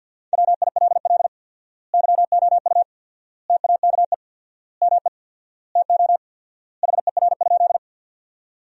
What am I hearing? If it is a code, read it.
Morse code, heard as a keyed tone, about 50 wpm, 700 Hz. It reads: WILL YOU TAKE ME TO HER?
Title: 4x Word-Spacing - Sentences from Top 200 Words - 50wpm